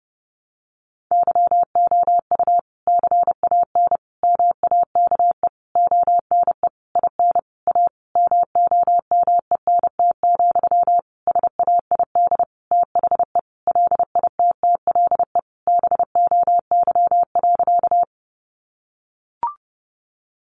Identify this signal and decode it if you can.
Morse code, heard as a keyed tone, about 30 wpm, 700 Hz. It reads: YOU CAN MAKE ONE IN A MOMENT, SAID THE LITTLE BOY.
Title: Sentences from Top 1000 Words 30wpm